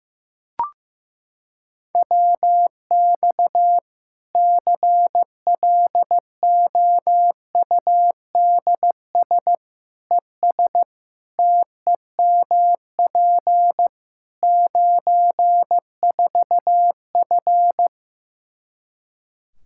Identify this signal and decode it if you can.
Morse code, heard as a keyed tone, about 15 wpm, 700 Hz. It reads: WX CLOUDS ES TEMP 94F
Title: Practice QSO - Exchange 2 - Info - 15wpm